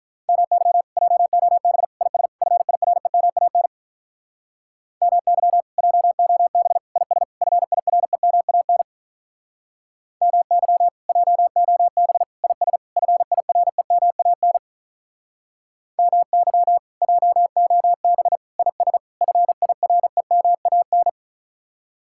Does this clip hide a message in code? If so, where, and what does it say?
Yes: Morse code from 0.3 to 21.1 s, heard as a keyed tone, about 48 wpm, 700 Hz. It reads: MY JOB IS FIREMAN MY JOB IS FIREMAN MY JOB IS FIREMAN MY JOB IS FIREMAN